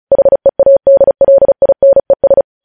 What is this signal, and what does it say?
Morse code, heard as a keyed tone, about 36 wpm, 550 Hz. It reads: HEADLINES
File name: podcast / News headlines-1x-35x35